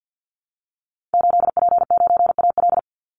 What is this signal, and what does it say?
Morse code, heard as a keyed tone, about 49 wpm, 700 Hz. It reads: 8P9AL
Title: Call Signs Repeated 3x - 50wpm